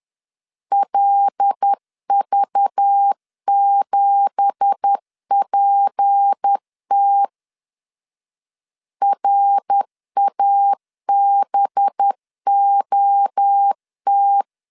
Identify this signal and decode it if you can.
Morse code, heard as a keyed tone, about 10 wpm, 800 Hz. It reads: LV7PT RABOT
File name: CW / DET / 1 9 1500C S B 10wpm 4439